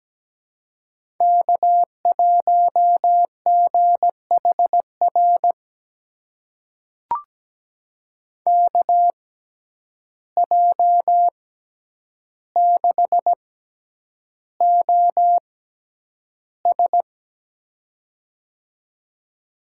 Morse code, heard as a keyed tone, about 17 wpm, spaced as for 9 wpm, 700 Hz. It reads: K1GHR KJ6OS
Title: USA Call Signs - 5 Characters - 6x Character Spacing - 17wpm